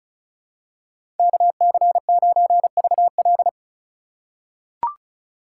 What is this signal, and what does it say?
Morse code, heard as a keyed tone, about 36 wpm, 700 Hz. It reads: KC9VL
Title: USA Call Signs - 5 Characters - 2x Character Spacing - 35wpm